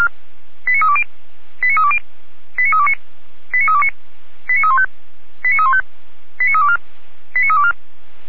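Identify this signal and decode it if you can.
Five-tone selective call (ZVEI), heard as quick runs of five tones, tones of about 70 ms. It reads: 89320 89320 89329 89329 89315 89315 89324 89324
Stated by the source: ZVEI five-tone call 89502 received by SDR, logged by Feuerwehr Orlamünde fire brigade.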